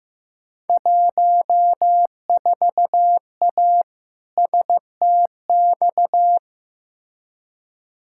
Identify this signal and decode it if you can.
Morse code, heard as a keyed tone, about 15 wpm, 700 Hz. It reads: 14A STX
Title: Field Day Exchanges Encore - 15wpm